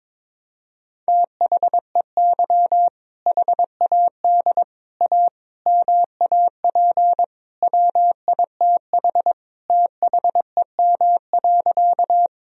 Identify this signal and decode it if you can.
Morse code, heard as a keyed tone, about 22 wpm, 700 Hz. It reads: THEY HAD A MAP WITH THEM.
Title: Sentences from Top 1000 Words - Review Part 2 - 22wpm